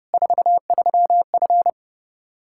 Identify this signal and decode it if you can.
Morse code, heard as a keyed tone, about 30 wpm, 700 Hz. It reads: 43F
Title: Practice QSO - Exchange 2 - Info - 30wpm